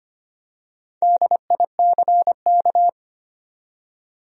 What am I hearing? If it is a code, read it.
Morse code, heard as a keyed tone, about 25 wpm, 700 Hz. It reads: DICK
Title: USA Common Names - Max 5 Letters - 25wpm